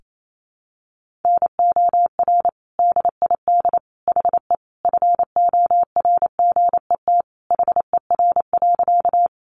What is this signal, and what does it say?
Morse code, heard as a keyed tone, about 28 wpm, 700 Hz. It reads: NOR DID HE FORGET HER.
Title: Sentences from Top 1500 Words 28wpm